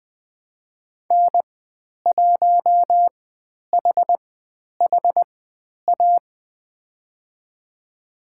Morse code, heard as a keyed tone, about 20 wpm, spaced as for 10 wpm, 700 Hz. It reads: N1HHA
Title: 095 Lesson 16g - Letter c - Call Signs 20wpm